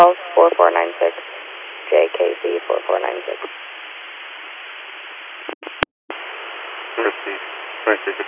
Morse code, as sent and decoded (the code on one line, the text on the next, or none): none